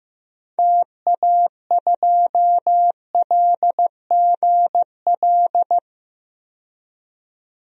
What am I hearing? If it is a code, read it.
Morse code, heard as a keyed tone, about 15 wpm, 700 Hz. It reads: TA2LGL
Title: DXCC-100 Call Signs - 6 Characters - 15wpm